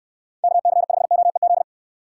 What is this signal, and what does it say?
Morse code, heard as a keyed tone, about 67 wpm, 700 Hz. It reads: KC5OEZ